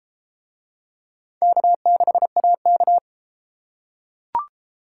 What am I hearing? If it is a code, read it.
Morse code, heard as a keyed tone, about 30 wpm, 700 Hz. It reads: K6AK